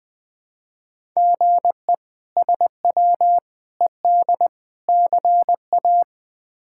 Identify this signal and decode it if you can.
Morse code, heard as a keyed tone, about 20 wpm, 700 Hz. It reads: GE SW ED CA
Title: K1USN SST QSO Practice - 20wpm